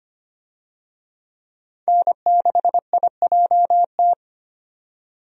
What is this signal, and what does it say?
Morse code, heard as a keyed tone, about 25 wpm, 690 Hz. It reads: N6IJT
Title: USA Call Signs - 5 Characters - 25wpm